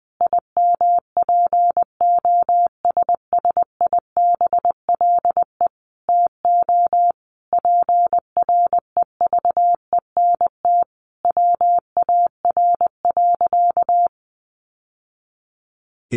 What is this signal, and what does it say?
Morse code, heard as a keyed tone, about 20 wpm, 700 Hz. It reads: IMPOSSIBLE TO PREVENT WAR.
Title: Sentences from Top 1100 Words 20wpm